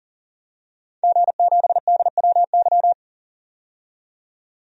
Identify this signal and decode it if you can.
Morse code, heard as a keyed tone, about 40 wpm, 700 Hz. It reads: G7DWY
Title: DXCC-100 Call Signs - 5 Characters - 40wpm